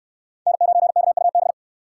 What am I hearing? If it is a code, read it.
Morse code, heard as a keyed tone, about 68 wpm, 700 Hz. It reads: N9KUB